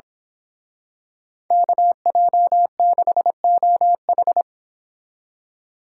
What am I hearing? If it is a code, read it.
Morse code, heard as a keyed tone, about 26 wpm, 700 Hz. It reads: KJ6OH
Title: Speed Racing - USA 2x2 Call Signs 22wpm